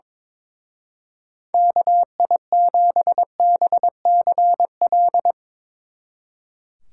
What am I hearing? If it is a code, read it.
Morse code, heard as a keyed tone, about 22 wpm, 700 Hz. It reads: KI7BCL
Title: Speed Racing - USA 2x3 Call Signs 22wpm